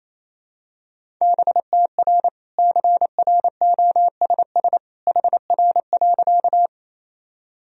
Morse code, heard as a keyed tone, about 28 wpm, 700 Hz. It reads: BTR CROSS HR.
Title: Sentences from Top 400 Words and Abbreviations 28wpm